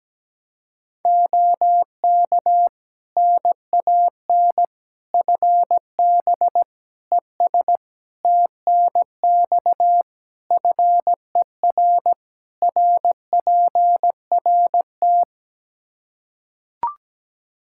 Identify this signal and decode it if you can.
Morse code, heard as a keyed tone, about 17 wpm, 700 Hz. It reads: OK NAN FB ES TNX FER RPRT
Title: Practice QSO - Exchange 2 - Info - 17wpm